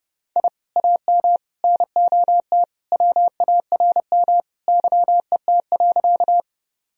Morse code, heard as a keyed tone, about 32 wpm, 700 Hz. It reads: I AM NOT WARM YET.
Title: Sentences from Top 1000 Words - Review Part 2 - 30wpm